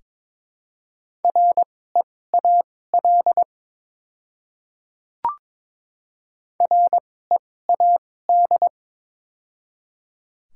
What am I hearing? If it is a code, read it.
Morse code, heard as a keyed tone, about 22 wpm, spaced as for 16 wpm, 700 Hz. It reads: REAL READ
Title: 2x Character Spacing - Top 300 Words - 22wpm